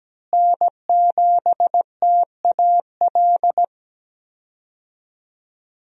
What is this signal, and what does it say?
Morse code, heard as a keyed tone, about 17 wpm, 700 Hz. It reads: N7TAL
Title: USA Call Signs - 5 Characters - Repeated 2x - 17wpm